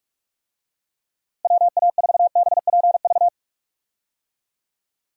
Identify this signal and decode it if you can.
Morse code, heard as a keyed tone, about 46 wpm, 700 Hz. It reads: WA4BPV
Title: USA Call Signs - 6 Characters - Repeated 3x - 45wpm